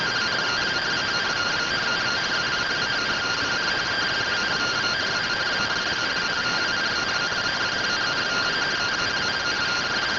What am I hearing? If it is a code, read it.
Two-tone teleprinter signal, about 170 Hz shift.